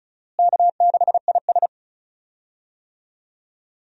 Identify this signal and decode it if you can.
Morse code, heard as a keyed tone, about 36 wpm, 700 Hz. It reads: K6IS
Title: USA Call Signs - 4 Characters - Repeated 3x - 35wpm